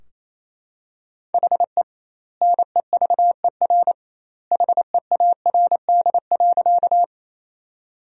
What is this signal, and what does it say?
Morse code, heard as a keyed tone, about 27 wpm, 700 Hz. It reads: HE NEVER HEARD.
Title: 2x Word-Spacing - Sentences from Top 400 Words - 28wpm